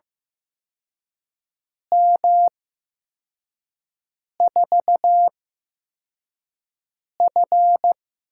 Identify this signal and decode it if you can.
Morse code, heard as a keyed tone, about 15 wpm, spaced as for 4 wpm, 700 Hz. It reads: M4F
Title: USA Call Signs - 6 Characters - 8x Character Spacing - 15wpm